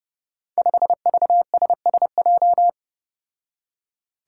Morse code, heard as a keyed tone, about 30 wpm, 700 Hz. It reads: 5VSSJ
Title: Five Challenging Letter-Number Combinations 30wpm